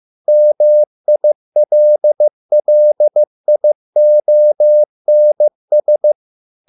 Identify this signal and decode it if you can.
Morse code, heard as a keyed tone, about 15 wpm, 600 Hz. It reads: MILLIONS